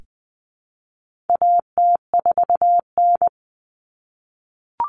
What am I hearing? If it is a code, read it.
Morse code, heard as a keyed tone, about 20 wpm, 700 Hz. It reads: AT4N
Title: Common Cut Numbers - 4-Digit 20wpm